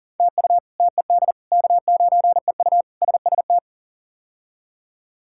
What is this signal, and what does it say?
Morse code, heard as a keyed tone, about 40 wpm, 700 Hz. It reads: TU TED K9EU SST
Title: K1USN SST QSO Practice - 40wpm